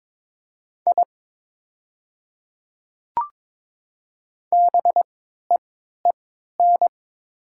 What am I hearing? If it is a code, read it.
Morse code, heard as a keyed tone, about 22 wpm, spaced as for 12 wpm, 700 Hz. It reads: I BEEN
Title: 3x Character Spacing - Top 100 Words - 22wpm